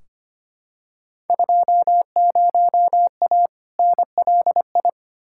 Morse code, heard as a keyed tone, about 25 wpm, 700 Hz. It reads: 20A NLI
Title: Field Day Exchanges Encore - 25wpm